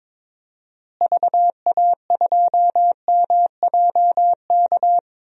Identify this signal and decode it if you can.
Morse code, heard as a keyed tone, about 22 wpm, 700 Hz. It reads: VA2MJK
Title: DXCC-100 Call Signs - 6 Characters - Repeated 3x - 22wpm